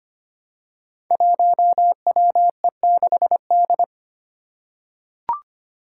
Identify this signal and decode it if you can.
Morse code, heard as a keyed tone, about 25 wpm, 700 Hz. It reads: 1WE6D
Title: Five Challenging Letter-Number Combinations 25wpm